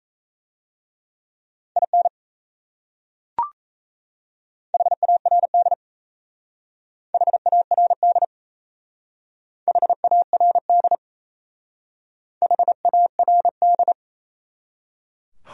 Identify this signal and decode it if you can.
Morse code, heard as a keyed tone, about 36 wpm, 700 Hz. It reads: IN HARD HARD HARD HARD